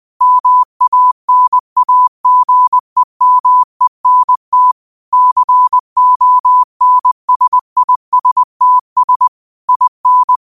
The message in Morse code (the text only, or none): MANAGEMENT CONSISTS IN